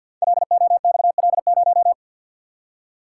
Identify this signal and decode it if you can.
Morse code, heard as a keyed tone, about 52 wpm, 700 Hz. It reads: LOXL0
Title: Code Groups - 5 Letter:Numbers - 50wpm